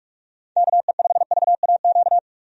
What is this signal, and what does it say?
Morse code, heard as a keyed tone, about 44 wpm, 700 Hz. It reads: KE5VAQ
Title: USA Call Signs - 6 Characters - 45wpm